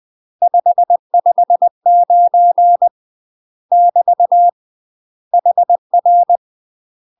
Morse code, heard as a keyed tone, about 20 wpm, 700 Hz. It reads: 559 <BT> HR
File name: UK MASTER QSOs TRAINING / 20WPM / UK MASTER QSOs 0059